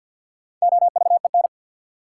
Morse code, heard as a keyed tone, about 53 wpm, 700 Hz. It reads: OVEN
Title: Sentences from Top 1200 Words - New Word Drill Alt - 50wpm